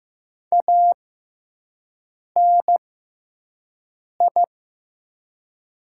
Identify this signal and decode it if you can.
Morse code, heard as a keyed tone, about 15 wpm, spaced as for 5 wpm, 700 Hz. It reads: ANI
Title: 6x Character Spacing - Top 300 Words - 15wpm